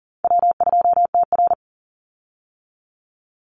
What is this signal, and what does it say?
Morse code, heard as a keyed tone, about 40 wpm, 700 Hz. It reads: W2TR
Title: Call Signs - 40wpm